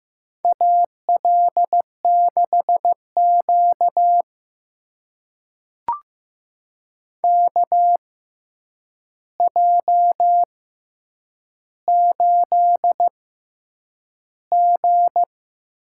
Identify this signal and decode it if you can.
Morse code, heard as a keyed tone, about 15 wpm, spaced as for 8 wpm, 700 Hz. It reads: AL6Q KJ8G
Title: USA Call Signs - 4 Characters - 6x Character Spacing - 15wpm